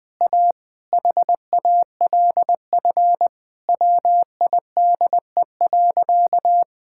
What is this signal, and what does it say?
Morse code, heard as a keyed tone, about 20 wpm, 700 Hz. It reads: A HALF WIDE.
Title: Sentences from Top 1000 Words - Review Part 6 - 20wpm